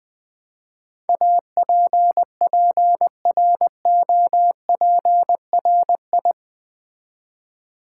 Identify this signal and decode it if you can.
Morse code, heard as a keyed tone, about 20 wpm, 700 Hz. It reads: APPROPRI
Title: Build A Word - 4 Letters At A Time - 11-Letter Words - 20wpm